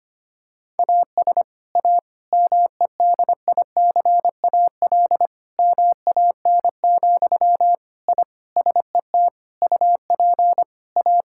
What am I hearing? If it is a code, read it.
Morse code, heard as a keyed tone, about 25 wpm, 700 Hz. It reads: AS A MEDICAL MAN, I SET UP A
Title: Sentences from Top 1000 Words - Review Part 5 - 25wpm